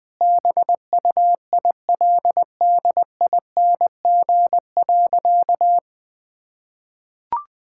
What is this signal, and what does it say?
Morse code, heard as a keyed tone, about 20 wpm, 700 Hz. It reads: BUILDING.
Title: Sentences from Top 1000 Words - Review Part 5 - 20wpm